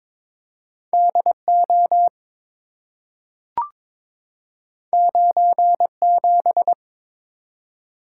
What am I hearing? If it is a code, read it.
Morse code, heard as a keyed tone, about 22 wpm, 700 Hz. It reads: DO 97
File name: practice / 22wpm / Two Letter-Number Combinations - 22wpm